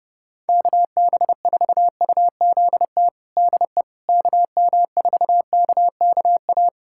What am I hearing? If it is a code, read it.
Morse code, heard as a keyed tone, about 30 wpm, 700 Hz. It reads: KB4UZT DE KM4KKA
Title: USA Callsign DE Callsign - 6 Characters - 30wpm